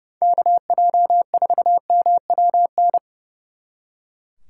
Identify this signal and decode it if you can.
Morse code, heard as a keyed tone, about 37 wpm, 700 Hz. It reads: KJ4MWN
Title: Speed Racing - USA Call Signs 30wpm